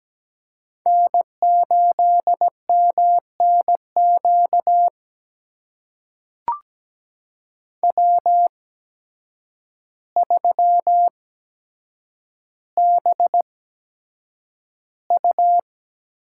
Morse code, heard as a keyed tone, about 17 wpm, spaced as for 8 wpm, 700 Hz. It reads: N8MNQ W3BU
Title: USA Call Signs - 8x Character Spacing - 17wpm